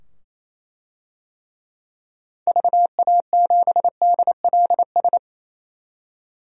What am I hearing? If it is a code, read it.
Morse code, heard as a keyed tone, about 28 wpm, 700 Hz. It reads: VA7DLS
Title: DXCC-100 Call Signs - 6 Characters - 28wpm